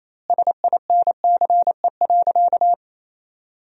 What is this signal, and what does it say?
Morse code, heard as a keyed tone, about 28 wpm, 700 Hz. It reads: SINCE.
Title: 2x Word-Spacing - Sentences from Top 300 Words - 28wpm